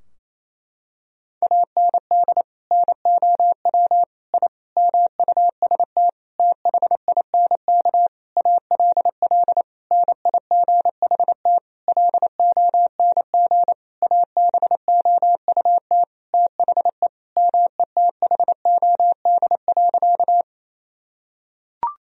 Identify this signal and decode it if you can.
Morse code, heard as a keyed tone, about 28 wpm, 700 Hz. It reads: AND NOW I MUST THINK ALL NIGHT LONG ABOUT THE METHOD.
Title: Sentences from Top 1000 Words - Review Part 3 - 28wpm